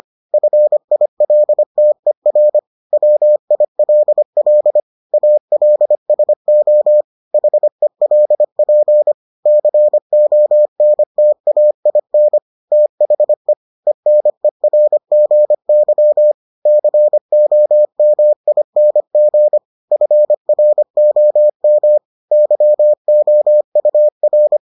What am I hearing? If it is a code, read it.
Morse code, heard as a keyed tone, about 25 wpm, 600 Hz. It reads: FILTER WILL ALSO HELP CONTAIN THE ENERGY COMING FROM YOUR